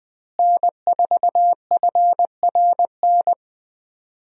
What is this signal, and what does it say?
Morse code, heard as a keyed tone, about 20 wpm, 700 Hz. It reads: N4FRN
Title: USA Call Signs - 5 Characters - Repeated 3x - 20wpm